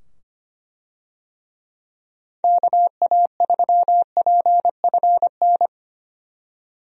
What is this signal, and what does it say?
Morse code, heard as a keyed tone, about 25 wpm, 700 Hz. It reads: KA3PFN
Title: USA Call Signs - 6 Characters - 25wpm